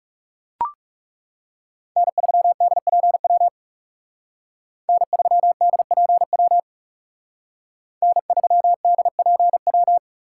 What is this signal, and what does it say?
Morse code, heard as a keyed tone, about 35 wpm, 700 Hz. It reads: N3DPW N3DPW N3DPW
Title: Speed Racing - USA 1x3 Call Signs 30wpm